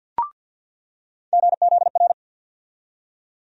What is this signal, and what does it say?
Morse code, heard as a keyed tone, about 47 wpm, 700 Hz. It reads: GZR
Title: USA Call Sign Postfixes - 3 Letters 50wpm